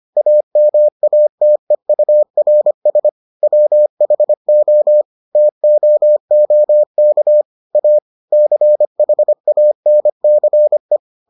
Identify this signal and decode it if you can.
Morse code, heard as a keyed tone, about 25 wpm, 600 Hz. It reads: AMATEURS WHO TOOK A CHANCE